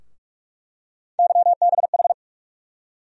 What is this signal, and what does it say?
Morse code, heard as a keyed tone, about 46 wpm, 700 Hz. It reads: YBH
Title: USA Call Sign Postfixes - 3 Letters 45wpm